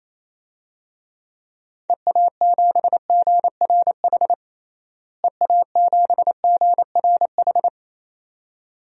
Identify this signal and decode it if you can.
Morse code, heard as a keyed tone, about 28 wpm, 700 Hz. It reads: EA7GRH EA7GRH
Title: DXCC-100 Call Signs - 6 Characters - Repeated 2x - 28wpm